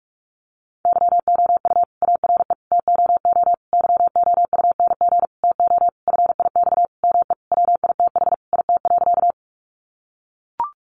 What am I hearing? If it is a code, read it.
Morse code, heard as a keyed tone, about 44 wpm, 700 Hz. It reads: YOU ARE TOO YOUNG TO FIX ME WITH IT.
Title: Sentences from Top 1400 Words 45wpm